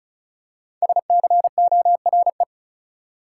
Morse code, heard as a keyed tone, about 36 wpm, 700 Hz. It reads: SCORE
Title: Five Letter Words - 35wpm